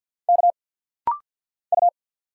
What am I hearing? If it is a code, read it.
Morse code, heard as a keyed tone, about 50 wpm, 700 Hz. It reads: K U
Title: Single Letters - Rapid-Fire 50wpm